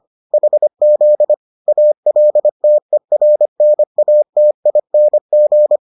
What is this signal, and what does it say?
Morse code, heard as a keyed tone, about 25 wpm, 600 Hz. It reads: HZ ALTERNATING